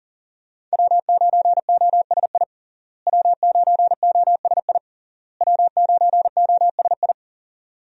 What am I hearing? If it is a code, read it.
Morse code, heard as a keyed tone, about 40 wpm, 700 Hz. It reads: W9OSI W9OSI W9OSI
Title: USA Call Signs - 5 Characters - Repeated 3x - 40wpm